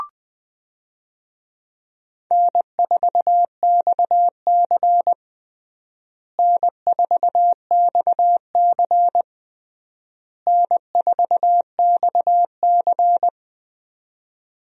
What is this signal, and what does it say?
Morse code, heard as a keyed tone, about 21 wpm, 700 Hz. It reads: N4XC N4XC N4XC
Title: USA Call Signs - 4 Characters - Repeated 3x - 20wpm